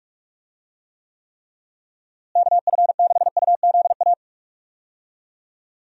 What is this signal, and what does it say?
Morse code, heard as a keyed tone, about 41 wpm, 700 Hz. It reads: KF6UZA
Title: USA Call Signs - 6 Characters - Repeated 3x - 45wpm